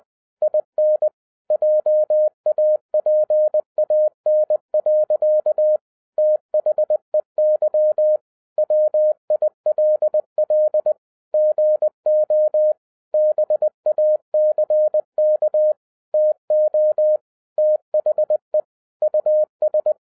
Morse code, heard as a keyed tone, about 20 wpm, 610 Hz. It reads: IN JAPAN. THEY WILL GO BACK TO THE US
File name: 2011-07-mp3 / n48 JOHN K3TN 20wpm